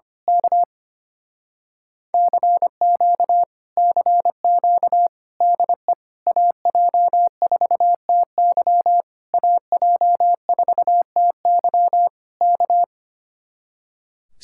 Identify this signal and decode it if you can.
Morse code, heard as a keyed tone, about 25 wpm, 700 Hz. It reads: K CQ CQ DE AJ4TY AJ4TY K
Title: Speed-Racing Common QSO Phrases Encore - 25wpm